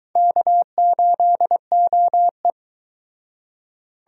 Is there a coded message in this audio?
Yes: Morse code from 0.2 to 2.5 s, heard as a keyed tone, about 24 wpm, 700 Hz. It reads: K8OE